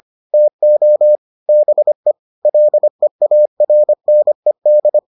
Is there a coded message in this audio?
Yes: Morse code from 0.3 to 5.0 s, heard as a keyed tone, about 25 wpm, 600 Hz. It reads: TO BE LEARNED